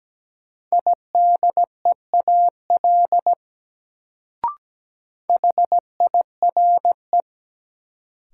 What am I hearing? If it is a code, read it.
Morse code, heard as a keyed tone, about 17 wpm, 700 Hz. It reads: IDEAL HIRE
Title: Sentences from Top 1400 Words - New Word Drill Alt - 17wpm